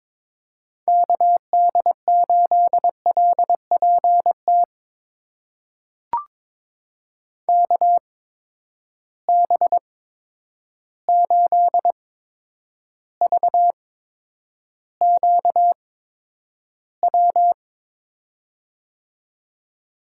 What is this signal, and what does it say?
Morse code, heard as a keyed tone, about 22 wpm, spaced as for 9 wpm, 700 Hz. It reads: KD8LPT KB8VQW
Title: USA Call Signs - 8x Character Spacing - 22wpm